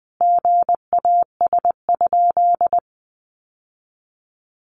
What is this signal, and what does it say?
Morse code, heard as a keyed tone, about 20 wpm, 700 Hz. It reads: GAS?
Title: Sentences from Top 1100 Words 20wpm